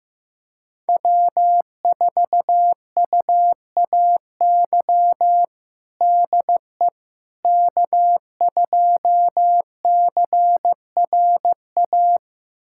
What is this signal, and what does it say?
Morse code, heard as a keyed tone, about 15 wpm, 700 Hz. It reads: W4UAY DE K2CRA
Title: USA Callsign DE Callsign - 5 Characters - 15wpm